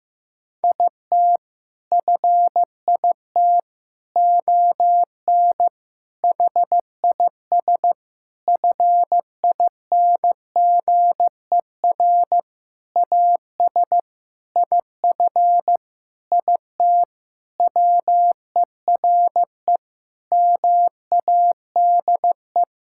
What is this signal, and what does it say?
Morse code, heard as a keyed tone, about 15 wpm, 700 Hz. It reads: IT FIT ON HIS FINGER AS IF IT WERE MADE
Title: Sentences from Top 1600 Words 15wpm